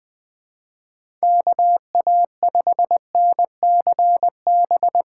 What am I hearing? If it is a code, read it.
Morse code, heard as a keyed tone, about 20 wpm, 700 Hz. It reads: KA5NCB
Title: USA Call Signs - 6 Characters - 6x Character Spacing - 20wpm